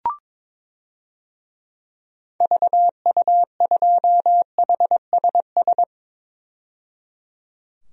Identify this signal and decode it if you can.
Morse code, heard as a keyed tone, about 22 wpm, 700 Hz. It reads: VU2HSS
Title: DXCC-100 Call Signs - 6 Characters - 22wpm